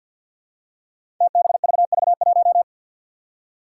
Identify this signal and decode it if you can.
Morse code, heard as a keyed tone, about 53 wpm, 700 Hz. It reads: TBVV1
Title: Five Challenging Letter-Number Combinations 50wpm